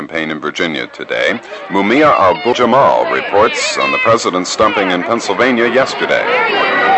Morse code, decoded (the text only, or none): none